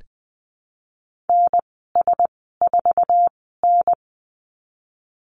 Morse code, heard as a keyed tone, about 20 wpm, spaced as for 15 wpm, 700 Hz. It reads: NS4N
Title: 147 Lesson 24g - Letter y - Call Signs 20wpm